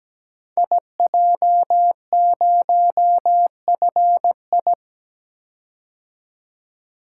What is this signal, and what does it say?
Morse code, heard as a keyed tone, about 17 wpm, 700 Hz. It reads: IJ0FI
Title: Code Groups - 5 Letter:Numbers - 17wpm